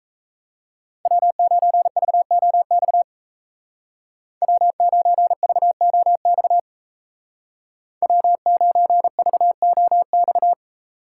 Morse code, heard as a keyed tone, about 36 wpm, 700 Hz. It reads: W9VOX W9VOX W9VOX